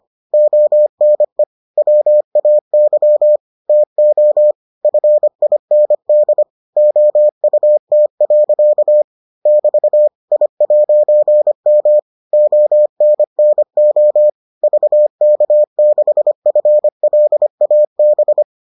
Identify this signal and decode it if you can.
Morse code, heard as a keyed tone, about 25 wpm, 600 Hz. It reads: ONE WAY TO FIND OUT. <BT> I'M ONNO VK6FLAB